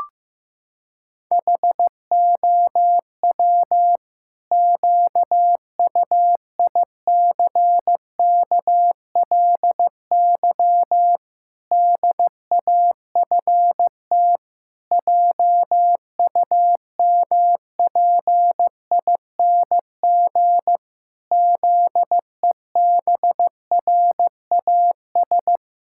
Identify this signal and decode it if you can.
Morse code, heard as a keyed tone, about 15 wpm, 700 Hz. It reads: HOW QUICKLY DAFT JUMPING ZEBRAS
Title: Pangram Challenge v2 - 15wpm